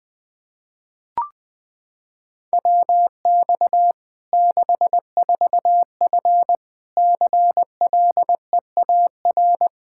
Morse code, heard as a keyed tone, about 20 wpm, 700 Hz. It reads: WX 64F CLEAR
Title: Practice QSO - Exchange 2 - Info - 20wpm